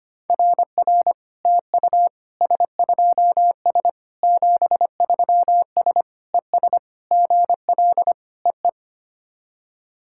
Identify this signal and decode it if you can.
Morse code, heard as a keyed tone, about 25 wpm, 700 Hz. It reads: RR TU S2S 73S ES GL EE
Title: SOTA QSO Practice 25wpm